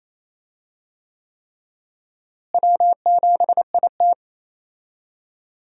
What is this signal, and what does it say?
Morse code, heard as a keyed tone, about 28 wpm, 700 Hz. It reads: W7IT
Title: USA Call Signs - 4 Characters - Repeated 3x - 28wpm